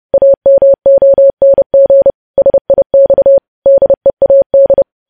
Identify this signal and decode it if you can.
Morse code, heard as a keyed tone, about 30 wpm, 550 Hz. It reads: AMONG SIX DEAD
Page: news headlines in morse code at 30 wpm